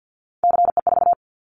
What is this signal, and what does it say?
Morse code, heard as a keyed tone, about 48 wpm, 700 Hz. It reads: CE4
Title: Callsign Prefixes for Top 100 DXCC - 50wpm